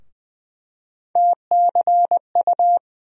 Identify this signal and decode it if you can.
Morse code, heard as a keyed tone, about 20 wpm, 700 Hz. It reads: TCU
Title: USA Call Sign Postfixes - 3 Letters 20wpm